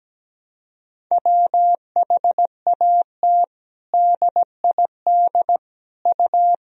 Morse code, heard as a keyed tone, about 17 wpm, 700 Hz. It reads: WHAT DID U
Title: Sentences from Top 200 Words and Abbreviations 17wpm